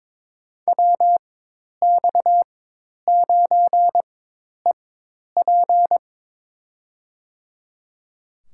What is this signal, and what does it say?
Morse code, heard as a keyed tone, about 21 wpm, spaced as for 10 wpm, 700 Hz. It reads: WX9EP
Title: USA Call Signs - 5 Characters - 4x Character Spacing - 22wpm